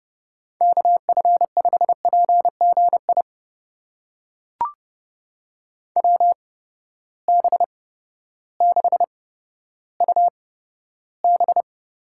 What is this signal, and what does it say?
Morse code, heard as a keyed tone, about 30 wpm, spaced as for 14 wpm, 700 Hz. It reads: KF5PGI WB6UB
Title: USA Call Signs - 6 Characters - 8x Character Spacing - 30wpm